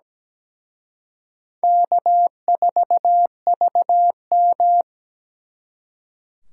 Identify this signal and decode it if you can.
Morse code, heard as a keyed tone, about 20 wpm, 700 Hz. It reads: K4VM
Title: Speed Racing - USA 1x2 Call Signs 17wpm